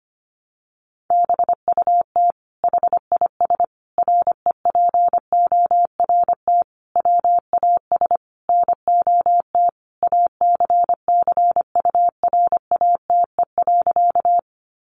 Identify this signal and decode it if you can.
Morse code, heard as a keyed tone, about 25 wpm, 700 Hz. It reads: BUT HIS REPORT WAS NOT ACCURATE.
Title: Sentences from Top 1200 Words 25wpm